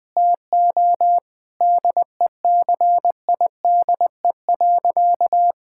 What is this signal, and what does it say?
Morse code, heard as a keyed tone, about 20 wpm, 700 Hz. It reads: TO DECIDE.
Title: Sentences with 8-Word Phrases - Spoken Beginning - 20wpm